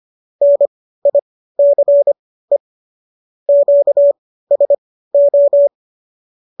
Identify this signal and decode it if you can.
Morse code, heard as a keyed tone, about 25 wpm, spaced as for 15 wpm, 570 Hz. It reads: NICE QSO